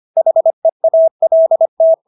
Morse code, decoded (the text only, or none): HEALT